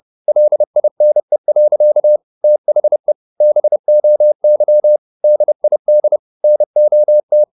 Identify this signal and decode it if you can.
Morse code, heard as a keyed tone, about 30 wpm, 600 Hz. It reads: LINE. THE BOY DID NOT